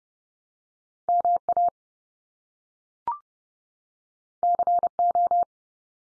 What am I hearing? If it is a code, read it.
Morse code, heard as a keyed tone, about 30 wpm, 700 Hz. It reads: MA CO
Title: US States 30WPM